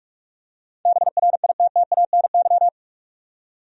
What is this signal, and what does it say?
Morse code, heard as a keyed tone, about 46 wpm, 700 Hz. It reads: BRITTANY
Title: USA Common Names - 45wpm